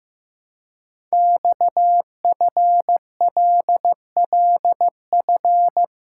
Morse code, heard as a keyed tone, about 15 wpm, 700 Hz. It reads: XFLLF
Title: Code Groups - 5 Letters - 15wpm